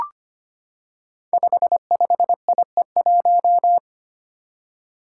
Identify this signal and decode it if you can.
Morse code, heard as a keyed tone, about 25 wpm, 700 Hz. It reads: 55IE1
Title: Five Challenging Letter-Number Combinations 25wpm